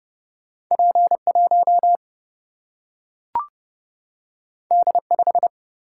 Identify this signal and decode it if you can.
Morse code, heard as a keyed tone, about 30 wpm, 700 Hz. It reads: P1 D5
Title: Single-Letter Single-Number 30wpm